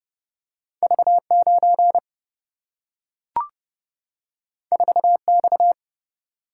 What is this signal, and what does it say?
Morse code, heard as a keyed tone, about 30 wpm, 700 Hz. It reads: V9 4X
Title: Single-Letter Single-Number 30wpm